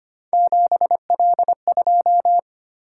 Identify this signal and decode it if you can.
Morse code, heard as a keyed tone, about 25 wpm, 700 Hz. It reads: 7L2
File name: Three Letter-Number Combinations - 25wpm